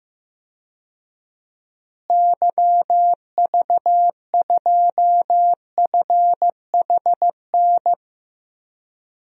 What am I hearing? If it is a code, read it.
Morse code, heard as a keyed tone, about 15 wpm, 700 Hz. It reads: YV2FHN
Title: DXCC-100 Call Signs - 6 Characters - Repeated 3x - 15wpm